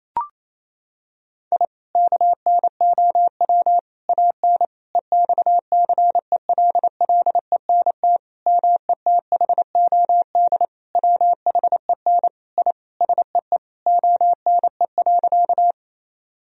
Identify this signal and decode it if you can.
Morse code, heard as a keyed tone, about 28 wpm, 700 Hz. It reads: I KNOW AN EXCELLENT METHOD WHEN I SEE ONE.
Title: Sentences from Top 1000 Words - Review Part 3 - 28wpm